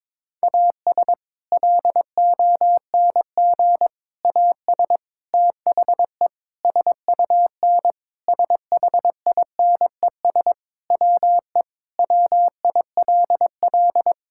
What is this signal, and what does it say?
Morse code, heard as a keyed tone, about 22 wpm, 700 Hz. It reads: AS LONG AS THE SUN SHINES WE WILL
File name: practice / 22wpm / Sentences with 4-Word Phrases - 22wpm